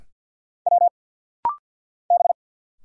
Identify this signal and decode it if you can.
Morse code, heard as a keyed tone, about 50 wpm, 700 Hz. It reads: W B
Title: Single Letters - Rapid-Fire 50wpm